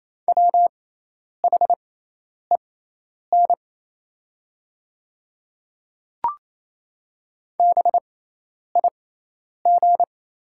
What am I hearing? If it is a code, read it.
Morse code, heard as a keyed tone, about 27 wpm, spaced as for 10 wpm, 700 Hz. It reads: WHEN BIG
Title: 6x Character Spacing - Top 200 Words - 28wpm